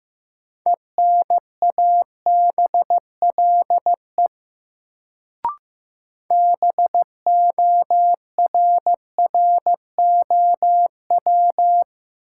Morse code, heard as a keyed tone, about 15 wpm, 700 Hz. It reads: ENABLE BORROW
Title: Sentences from Top 1500 Words - New Word Drill - 15wpm